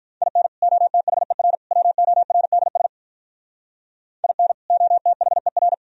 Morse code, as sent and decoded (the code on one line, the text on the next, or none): .. -. / --- - .... .-. / .-- --- .-. -.. ... / .. -. / --- - .... .-.
IN OTHER WORDS IN OTHER